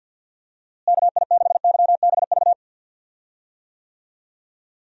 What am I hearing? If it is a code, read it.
Morse code, heard as a keyed tone, about 52 wpm, 700 Hz. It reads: KI6YBV
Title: USA Call Signs - 6 Characters - Repeated 3x - 50wpm